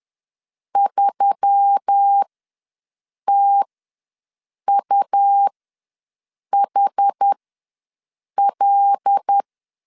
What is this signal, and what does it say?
Morse code, heard as a keyed tone, about 10 wpm, spaced as for 6 wpm, 800 Hz. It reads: 3TUHL
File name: LEZIONI / 7 GF9Z / 7 GF9Z ws 10wpm 3919